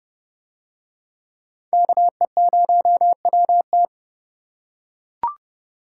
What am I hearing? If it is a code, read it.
Morse code, heard as a keyed tone, about 29 wpm, 700 Hz. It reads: KE0WT